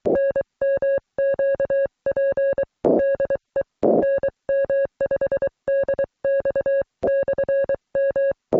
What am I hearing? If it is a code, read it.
Morse code, heard as a keyed tone, about 23 wpm, 550 Hz. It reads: NMQP DE NM5DX/M